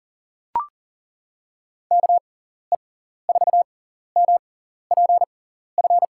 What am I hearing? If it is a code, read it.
Morse code, heard as a keyed tone, about 37 wpm, spaced as for 14 wpm, 700 Hz. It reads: KE4MPF
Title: USA Call Signs - 6 Characters - 6x Character Spacing - 40wpm